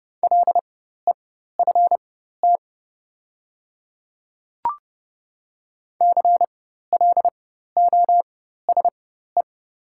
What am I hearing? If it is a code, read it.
Morse code, heard as a keyed tone, about 30 wpm, spaced as for 14 wpm, 700 Hz. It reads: LEFT CLOSE
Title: 4x Character Spacing - Top 300 Words - 30wpm